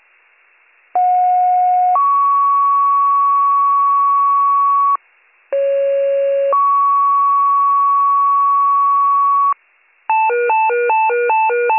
Two-tone page, heard as a one-second tone, then a three-second tone, about 730 then 1120 Hz, about 550 then 1120 Hz. It is heard from 0.9 to 9.5 s.